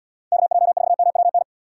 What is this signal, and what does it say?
Morse code, heard as a keyed tone, about 77 wpm, 700 Hz. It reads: KJ4MWN